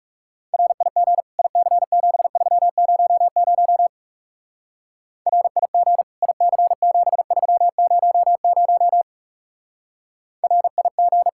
Morse code, heard as a keyed tone, about 35 wpm, 700 Hz. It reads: RIG IC7300 RIG IC7300 RIG
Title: Speed-Racing Common QSO Phrases - 30wpm